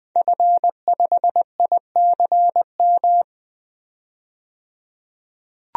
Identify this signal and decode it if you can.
Morse code, heard as a keyed tone, about 20 wpm, 700 Hz. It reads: F5ICM
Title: DXCC-100 Call Signs - 5 Characters - 20wpm